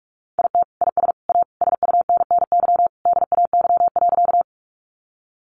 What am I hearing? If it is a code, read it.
Morse code, heard as a keyed tone, about 44 wpm, 700 Hz. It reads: IT IS A SUNNY DAY.
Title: Additional Easy Phrases 45wpm